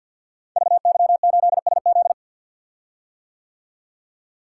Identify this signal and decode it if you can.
Morse code, heard as a keyed tone, about 49 wpm, 700 Hz. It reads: VY8SZ
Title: Code Groups - 5 Letter:Numbers - 50wpm